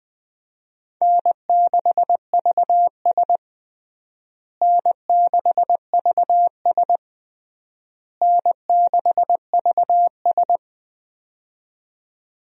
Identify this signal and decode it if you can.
Morse code, heard as a keyed tone, about 20 wpm, 700 Hz. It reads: N6VS N6VS N6VS
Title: USA Call Signs - 4 Characters - Repeated 3x - 20wpm